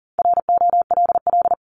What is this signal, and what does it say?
Morse code, heard as a keyed tone, about 40 wpm, 700 Hz. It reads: ROLL
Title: Top 1751-2000 Words 40wpm